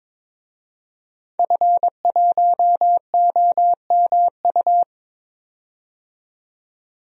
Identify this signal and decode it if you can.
Morse code, heard as a keyed tone, about 22 wpm, 700 Hz. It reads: F1OMU
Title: DXCC-100 Call Signs - 5 Characters - 22wpm